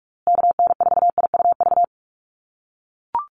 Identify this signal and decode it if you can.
Morse code, heard as a keyed tone, about 44 wpm, 700 Hz. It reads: KN4IUV
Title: USA Call Signs Repeated 3x 45wpm